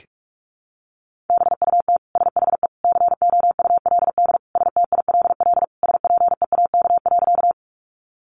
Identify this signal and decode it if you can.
Morse code, heard as a keyed tone, about 44 wpm, 700 Hz. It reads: BUT SHE COULD STILL SPEAK.
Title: Sentences from Top 1100 Words 45wpm